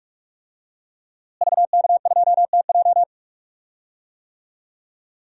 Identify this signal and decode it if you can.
Morse code, heard as a keyed tone, about 44 wpm, 700 Hz. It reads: VK2TJ
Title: DXCC-100 Call Signs - 5 Characters - Repeated 2x - 45wpm